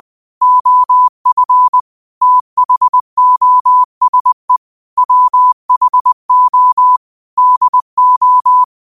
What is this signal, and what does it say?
Morse code, heard as a keyed tone, about 20 wpm, 1000 Hz. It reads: OF THOSE WHO DO